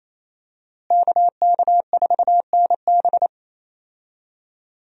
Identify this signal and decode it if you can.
Morse code, heard as a keyed tone, about 28 wpm, 700 Hz. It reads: KK4NB
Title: USA Call Signs - 5 Characters - Repeated 3x - 28wpm